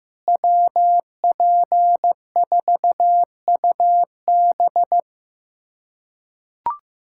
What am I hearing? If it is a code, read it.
Morse code, heard as a keyed tone, about 15 wpm, 700 Hz. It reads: WP4UB
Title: USA Callsign DE Callsign - 5 Characters - 15wpm